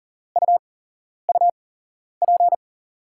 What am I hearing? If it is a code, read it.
Morse code, heard as a keyed tone, about 39 wpm, spaced as for 11 wpm, 700 Hz. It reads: UUP
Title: USA Call Signs - 8x Character Spacing - 40wpm